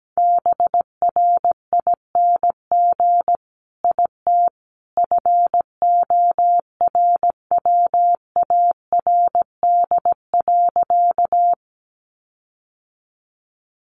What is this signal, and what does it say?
Morse code, heard as a keyed tone, about 17 wpm, 700 Hz. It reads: BRING IT FORWARD.
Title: Sentences from Top 1300 Words 17wpm